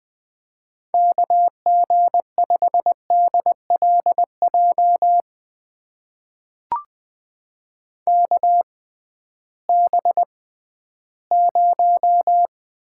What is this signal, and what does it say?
Morse code, heard as a keyed tone, about 20 wpm, spaced as for 13 wpm, 700 Hz. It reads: KG5DLJ KB0
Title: USA Call Signs - 6 Characters - 6x Character Spacing - 20wpm